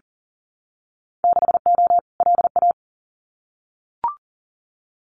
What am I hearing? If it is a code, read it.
Morse code, heard as a keyed tone, about 40 wpm, 700 Hz. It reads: 6O LA
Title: Winter Field Day Exchanges Encore 40wpm